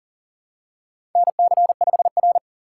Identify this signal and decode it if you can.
Morse code, heard as a keyed tone, about 43 wpm, 700 Hz. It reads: NC5R
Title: USA Call Signs - 4 Characters - 2x Character Spacing - 40wpm